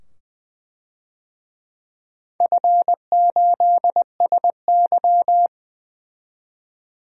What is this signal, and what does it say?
Morse code, heard as a keyed tone, about 20 wpm, 700 Hz. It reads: F8SY
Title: DXCC-100 Call Signs - 4 Characters - 20wpm